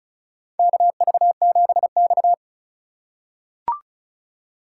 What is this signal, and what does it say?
Morse code, heard as a keyed tone, about 36 wpm, 700 Hz. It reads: KV7X
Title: USA Call Signs - 4 Characters - 3x Character Spacing - 35wpm